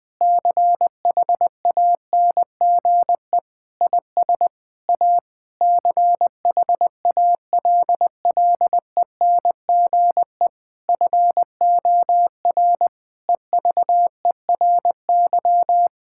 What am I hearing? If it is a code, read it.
Morse code, heard as a keyed tone, about 20 wpm, 700 Hz. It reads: CHANGE IS A CHALLENGE FOR EVERY
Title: Difficult Sentences with Hints - 20wpm